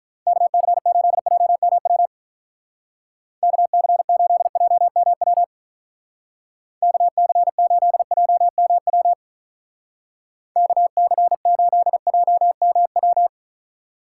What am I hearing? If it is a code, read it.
Morse code, heard as a keyed tone, about 42 wpm, 700 Hz. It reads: KC8JMW KC8JMW KC8JMW KC8JMW
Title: Speed Racing - USA Call Signs 35wpm